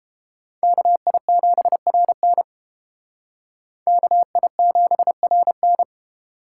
Morse code, heard as a keyed tone, about 30 wpm, 700 Hz. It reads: KI7RN KI7RN